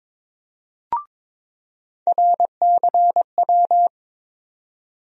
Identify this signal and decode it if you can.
Morse code, heard as a keyed tone, about 21 wpm, 700 Hz. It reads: RCW
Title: USA Call Sign Postfixes - 3 Letters 22wpm